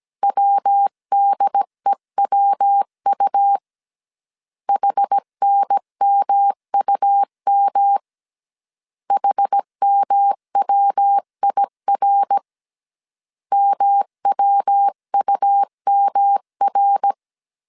Morse code, heard as a keyed tone, about 17 wpm, 800 Hz. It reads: WDEWU HNMUM HMWIR MWUMR
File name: CW / DET / 1 3 1500C S A 16wpm 2141